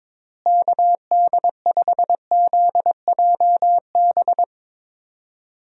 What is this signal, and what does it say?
Morse code, heard as a keyed tone, about 21 wpm, 700 Hz. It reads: KD5ZJB
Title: USA Call Signs - 6 Characters - Repeated 3x - 22wpm